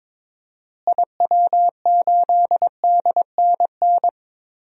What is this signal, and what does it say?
Morse code, heard as a keyed tone, about 22 wpm, 700 Hz. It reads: IW8DNN
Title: DXCC-100 Call Signs - 6 Characters - Repeated 3x - 22wpm